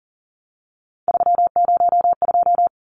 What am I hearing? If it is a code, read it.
Morse code, heard as a keyed tone, about 40 wpm, 700 Hz. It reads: 302
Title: Numbers - 3-Digit 40wpm